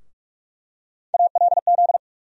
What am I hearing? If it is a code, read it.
Morse code, heard as a keyed tone, about 46 wpm, 700 Hz. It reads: ALZ
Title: USA Call Sign Postfixes - 3 Letters 45wpm